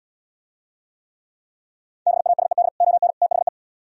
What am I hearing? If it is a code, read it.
Morse code, heard as a keyed tone, about 77 wpm, 700 Hz. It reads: BASED ON THE